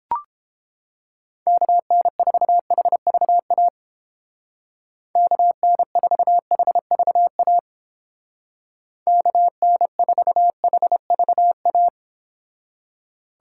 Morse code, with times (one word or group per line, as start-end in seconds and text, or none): 1.5-3.7 KN4HVA
5.2-7.6 KN4HVA
9.1-11.9 KN4HVA